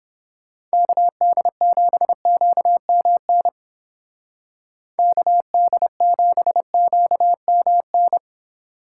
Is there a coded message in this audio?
Yes: Morse code from 0.7 to 8.2 s, heard as a keyed tone, about 29 wpm, 700 Hz. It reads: KD7QMN KD7QMN